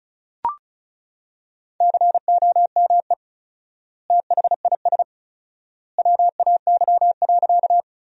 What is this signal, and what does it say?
Morse code, heard as a keyed tone, about 34 wpm, 700 Hz. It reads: COME THIS WAY.
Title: 4x Word-Spacing - Sentences from Top 100 Words - 35wpm